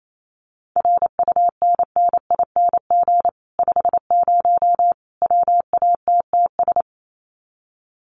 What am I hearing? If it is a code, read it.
Morse code, heard as a keyed tone, about 28 wpm, 700 Hz. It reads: RUNNING 50 WATTS
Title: Common QSO Phrases 28wpm